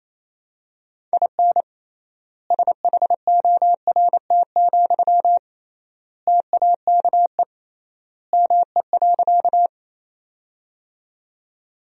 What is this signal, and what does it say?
Morse code, heard as a keyed tone, about 28 wpm, 700 Hz. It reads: IN SHORT, TAKE ME.
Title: 3x Word-Spacing - Sentences from Top 400 Words - 28wpm